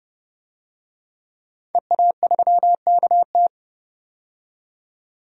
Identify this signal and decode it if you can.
Morse code, heard as a keyed tone, about 31 wpm, 700 Hz. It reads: EA3KT
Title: DXCC-100 Call Signs - 5 Characters - 30wpm